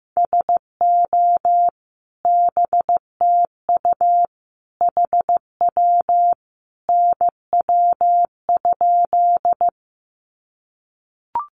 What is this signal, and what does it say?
Morse code, heard as a keyed tone, about 15 wpm, 700 Hz. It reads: SO BTU HW NW?
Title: Common QSO Phrases 15wpm